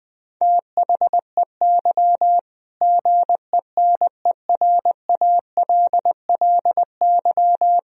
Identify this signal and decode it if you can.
Morse code, heard as a keyed tone, about 21 wpm, 700 Hz. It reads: THEY GENERALLY
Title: Sentences from Top 1000 Words - Review Part 4 - 20wpm